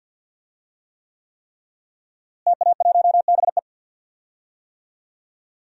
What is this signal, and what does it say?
Morse code, heard as a keyed tone, about 48 wpm, 700 Hz. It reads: TA1BE
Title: DXCC-100 Call Signs - 5 Characters - Repeated 3x - 50wpm